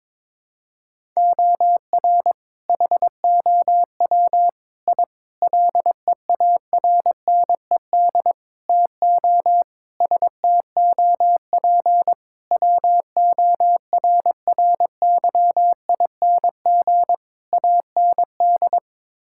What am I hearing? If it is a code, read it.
Morse code, heard as a keyed tone, about 21 wpm, 700 Hz. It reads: OR HOW I LEARNED TO STOP WORRYING AND
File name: practice / 22wpm / Sets of 8 Words - 22wpm